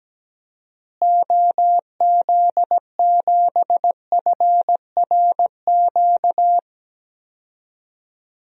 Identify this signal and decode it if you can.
Morse code, heard as a keyed tone, about 17 wpm, 700 Hz. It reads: OZ7FRQ
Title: DXCC-100 Call Signs - 6 Characters - Repeated 3x - 17wpm